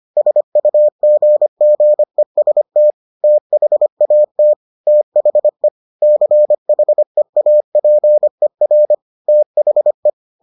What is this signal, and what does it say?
Morse code, heard as a keyed tone, about 25 wpm, 600 Hz. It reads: SUGGEST THAT THE CHEAPER THE